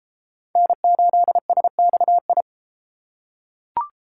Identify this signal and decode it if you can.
Morse code, heard as a keyed tone, about 32 wpm, 700 Hz. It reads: N8SXI